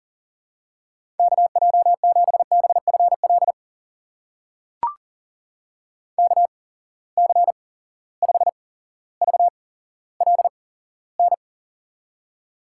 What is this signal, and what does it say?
Morse code, heard as a keyed tone, about 42 wpm, spaced as for 17 wpm, 700 Hz. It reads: KJ7BFL KC5VLN
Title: USA Call Signs - 6 Characters - 8x Character Spacing - 40wpm